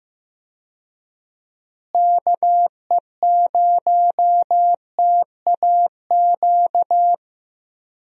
Morse code, heard as a keyed tone, about 15 wpm, 700 Hz. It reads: KE0TAQ
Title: USA Call Signs - 6 Characters - Repeated 2x - 15wpm